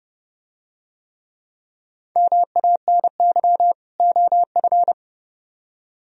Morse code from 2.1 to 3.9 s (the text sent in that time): MANY